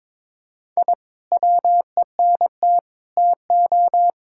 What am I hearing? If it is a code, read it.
Morse code, heard as a keyed tone, about 21 wpm, 700 Hz. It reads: I WENT TO
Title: Sentences from Top 1000 Words - Review Part 5 - 22wpm